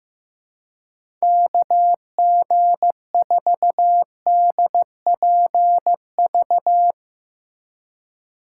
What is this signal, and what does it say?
Morse code, heard as a keyed tone, about 15 wpm, 700 Hz. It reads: KG4DPV
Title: USA Call Signs - 6 Characters - 2x Character Spacing - 15wpm